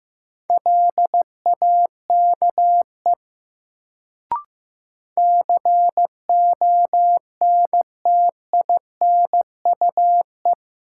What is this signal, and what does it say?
Morse code, heard as a keyed tone, about 15 wpm, 700 Hz. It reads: LAKE CONTINUE
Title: Sentences from Top 1100 Words - New Word Drill Alt - 15wpm